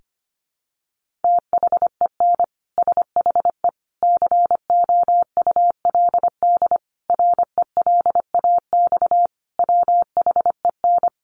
Morse code, heard as a keyed tone, about 25 wpm, 700 Hz. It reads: THEN SHE COULD RELAX WHEN
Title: Sentences from Top 1400 Words 25wpm